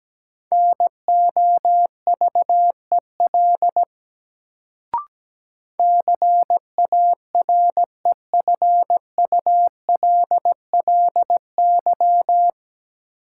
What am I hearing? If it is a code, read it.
Morse code, heard as a keyed tone, about 17 wpm, 700 Hz. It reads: NOVEL CAREFULLY
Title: Sentences from Top 1300 Words - New Word Drill Alt - 17wpm